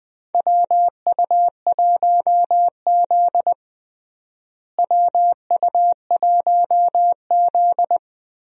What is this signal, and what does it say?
Morse code, heard as a keyed tone, about 20 wpm, 700 Hz. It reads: WU1Z WU1Z
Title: USA Call Signs - 4 Characters - Repeated 3x - 20wpm